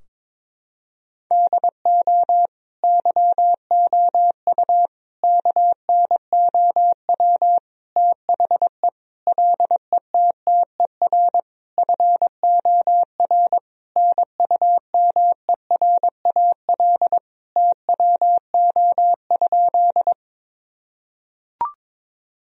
Morse code, heard as a keyed tone, about 22 wpm, 700 Hz. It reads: DO YOU KNOW THE LETTER FOR NUMERAL TWO?
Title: Sentences from Top 1000 Words - Review Part 2 - 22wpm